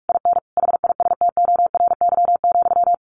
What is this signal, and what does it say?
Morse code, heard as a keyed tone, about 44 wpm, 700 Hz. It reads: IN HISTORY,
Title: A Sound Of Thunder 45wpm